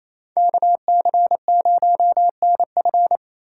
Morse code, heard as a keyed tone, about 36 wpm, 700 Hz. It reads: KC0NF